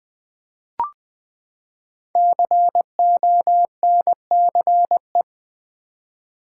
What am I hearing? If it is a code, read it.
Morse code, heard as a keyed tone, about 20 wpm, 700 Hz. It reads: CONCE
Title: Build A Word - 5 Letters At A Time - 10-Letter Words - 20wpm